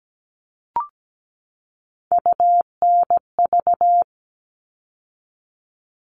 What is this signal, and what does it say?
Morse code, heard as a keyed tone, about 17 wpm, 700 Hz. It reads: UNV
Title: All Cut Numbers - 3-Digit 17wpm